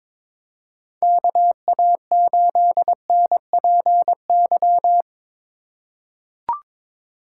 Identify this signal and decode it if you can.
Morse code, heard as a keyed tone, about 22 wpm, 700 Hz. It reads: KA8NPY
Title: USA Call Signs - 4x Character Spacing - 22wpm